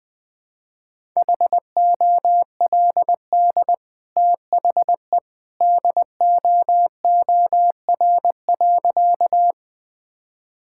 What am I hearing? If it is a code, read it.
Morse code, heard as a keyed tone, about 20 wpm, 700 Hz. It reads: HOLD THE DOOR.
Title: Easy Sentences with Hints - 20wpm